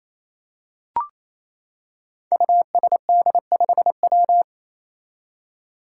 Morse code, heard as a keyed tone, about 28 wpm, 700 Hz. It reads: USD5W
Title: Five Challenging Letter-Number Combinations 28wpm